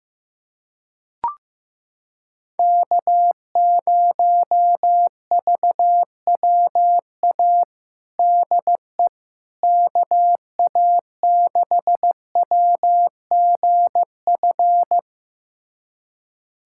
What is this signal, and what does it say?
Morse code, heard as a keyed tone, about 15 wpm, 700 Hz. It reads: K0VWA DE KA6WGF
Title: USA Callsign DE Callsign 15wpm